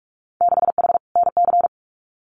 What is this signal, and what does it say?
Morse code, heard as a keyed tone, about 44 wpm, 690 Hz. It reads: 6H NC
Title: Winter Field Day Exchanges Encore 45wpm